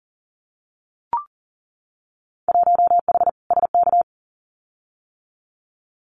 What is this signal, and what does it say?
Morse code, heard as a keyed tone, about 40 wpm, 700 Hz. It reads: 1H SK
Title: Winter Field Day Exchanges 40wpm